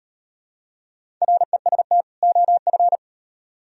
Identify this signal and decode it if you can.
Morse code, heard as a keyed tone, about 38 wpm, 700 Hz. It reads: REST OF